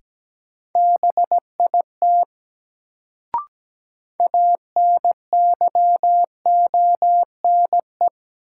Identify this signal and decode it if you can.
Morse code, heard as a keyed tone, about 17 wpm, 700 Hz. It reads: BIT ANYONE
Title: Sentences from Top 900 Words - New Word Drill - 17wpm